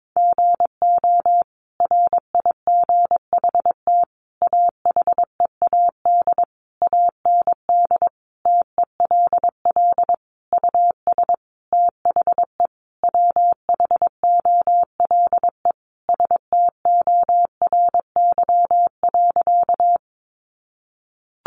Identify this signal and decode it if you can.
Morse code, heard as a keyed tone, about 22 wpm, 700 Hz. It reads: GO RIGHT AHEAD AND TELL US THE WHOLE STORY.
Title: Sentences from Top 1100 Words 22wpm